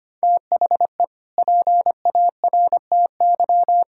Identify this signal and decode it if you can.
Morse code, heard as a keyed tone, about 25 wpm, 700 Hz. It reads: THE PARTY
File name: practice / 25wpm / Sentences with 2-Word Phrases - 25wpm